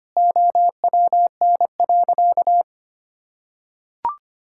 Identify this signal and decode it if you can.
Morse code, heard as a keyed tone, about 25 wpm, 700 Hz. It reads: OWN.
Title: Sentences from Top 1000 Words - Review Part 6 - 25wpm